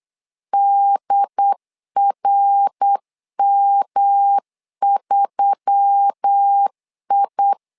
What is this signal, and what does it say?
Morse code, heard as a keyed tone, about 8 wpm, 800 Hz. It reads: DRM3I
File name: LEZIONI / 4 Q1S3 8wpm 3037